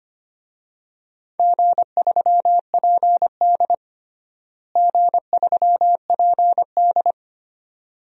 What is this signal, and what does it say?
Morse code, heard as a keyed tone, about 25 wpm, 700 Hz. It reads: G3PD G3PD
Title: DXCC-100 Call Signs - 4 Characters - Repeated 2x - 25wpm